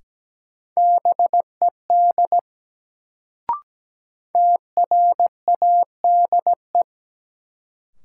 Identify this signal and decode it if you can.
Morse code, heard as a keyed tone, about 17 wpm, 700 Hz. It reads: BED TRADE
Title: Sets of 5 Words Encore - New Word Drill - 17wpm